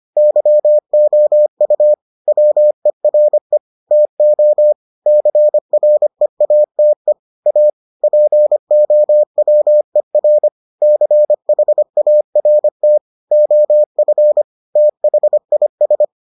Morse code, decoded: YOU WERE TO CREATE A POWER CHART OF THIS

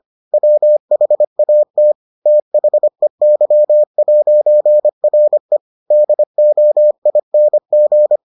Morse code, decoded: WHAT THEY'RE DOING

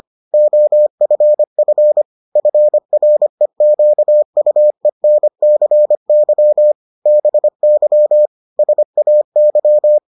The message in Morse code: OFF FREQUENCY BY SAY

